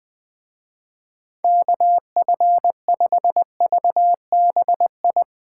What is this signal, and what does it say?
Morse code, heard as a keyed tone, about 20 wpm, 700 Hz. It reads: KF5VBI